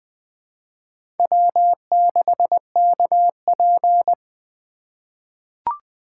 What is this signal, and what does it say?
Morse code, heard as a keyed tone, about 20 wpm, 700 Hz. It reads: W6KP
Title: USA Call Signs - 4 Characters - 2x Character Spacing - 20wpm